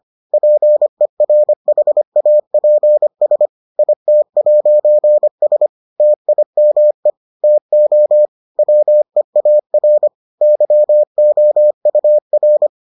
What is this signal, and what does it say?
Morse code, heard as a keyed tone, about 25 wpm, 600 Hz. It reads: PERHAPS IT'S TIME TO WEAR YOUR